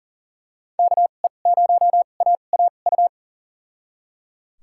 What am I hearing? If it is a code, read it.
Morse code, heard as a keyed tone, about 39 wpm, spaced as for 29 wpm, 700 Hz. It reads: KE0AAU
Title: USA Call Signs - 6 Characters - 2x Character Spacing - 40wpm